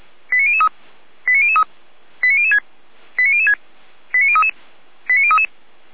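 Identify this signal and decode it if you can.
Five-tone selective call (ZVEI), heard as quick runs of five tones, tones of about 70 ms. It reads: 89003 89003 89006 89006 89033 89033
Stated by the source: ZVEI five-tone selective call, received by SDR.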